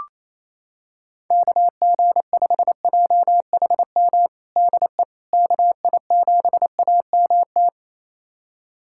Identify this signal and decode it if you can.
Morse code, heard as a keyed tone, about 28 wpm, 700 Hz. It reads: KG5JHM DE KI7AMT
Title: USA Callsign DE Callsign - 6 Characters - 28wpm